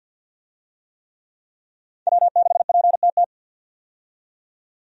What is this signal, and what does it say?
Morse code, heard as a keyed tone, about 52 wpm, 700 Hz. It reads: W6PTT
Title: USA Call Signs - 5 Characters - 50wpm